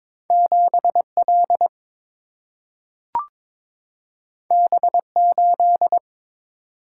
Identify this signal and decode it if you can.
Morse code, heard as a keyed tone, about 22 wpm, 700 Hz. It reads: 7 L B 8
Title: Single-Letter Single-Number 22wpm